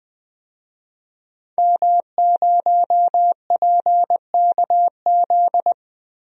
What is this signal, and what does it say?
Morse code, heard as a keyed tone, about 20 wpm, 700 Hz. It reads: M0PKZ
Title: DXCC-100 Call Signs - 5 Characters - 20wpm